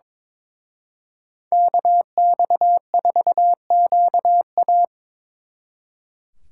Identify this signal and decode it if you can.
Morse code, heard as a keyed tone, about 25 wpm, 700 Hz. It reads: KX4QA
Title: Speed Racing - USA 2x2 Call Signs 22wpm